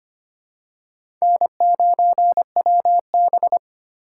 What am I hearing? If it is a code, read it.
Morse code, heard as a keyed tone, about 25 wpm, 700 Hz. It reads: N9WB
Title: USA Call Signs - 4 Characters - 8x Character Spacing - 25wpm